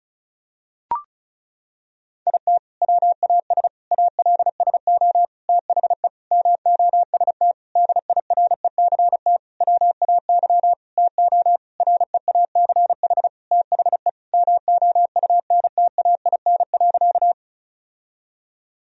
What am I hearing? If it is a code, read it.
Morse code, heard as a keyed tone, about 34 wpm, 700 Hz. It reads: IT WAS ALSO THE MOST DIRECT WAY TO REACH THE MOUNTAIN.
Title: Sentences from Top 1000 Words - Review Part 2 - 35wpm